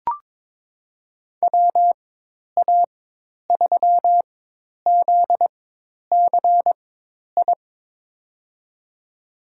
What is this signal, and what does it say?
Morse code, heard as a keyed tone, about 22 wpm, spaced as for 10 wpm, 700 Hz. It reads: WA3ZCI
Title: USA Call Signs - 4x Character Spacing - 22wpm